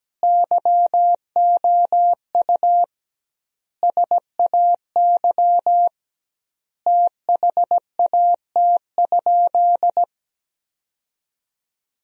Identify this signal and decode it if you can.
Morse code, heard as a keyed tone, about 17 wpm, 700 Hz. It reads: YOU SAY THAT?
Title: 2x Word-Spacing - Sentences from Top 200 Words - 17wpm